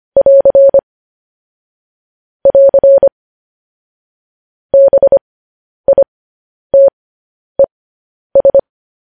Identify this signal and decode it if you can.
Morse code, heard as a keyed tone, about 25 wpm, spaced as for 10 wpm, 550 Hz. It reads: <AR> <AR> BITES
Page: news headlines in morse code at 10 wpm